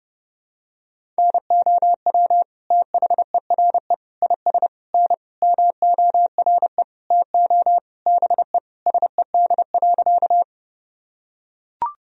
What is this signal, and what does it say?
Morse code, heard as a keyed tone, about 30 wpm, 700 Hz. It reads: NOW THERE IS N MORE TO BE SED.
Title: Sentences from Top 100 Words and Abbreviations 30wpm